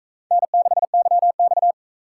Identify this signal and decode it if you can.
Morse code, heard as a keyed tone, about 40 wpm, 700 Hz. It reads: N6YX